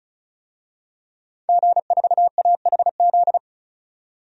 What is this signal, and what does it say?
Morse code, heard as a keyed tone, about 36 wpm, 700 Hz. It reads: G4AHZ
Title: DXCC-100 Call Signs - 5 Characters - 35wpm